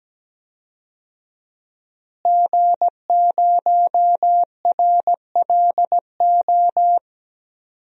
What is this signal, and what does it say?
Morse code, heard as a keyed tone, about 17 wpm, 700 Hz. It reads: G0RLO
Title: DXCC-100 Call Signs - 5 Characters - Repeated 3x - 17wpm